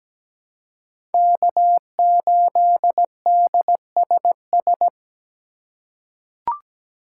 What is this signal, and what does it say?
Morse code, heard as a keyed tone, about 17 wpm, 700 Hz. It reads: K8DSS
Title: K1USN SST QSO Practice - 17wpm